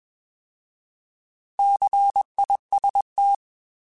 Morse code, heard as a keyed tone, about 21 wpm, 800 Hz. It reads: CIST